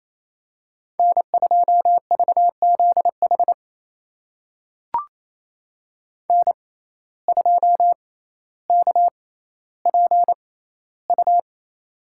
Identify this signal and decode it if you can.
Morse code, heard as a keyed tone, about 28 wpm, spaced as for 14 wpm, 700 Hz. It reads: N2VZH N2KPU
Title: USA Call Signs - 6x Character Spacing - 28wpm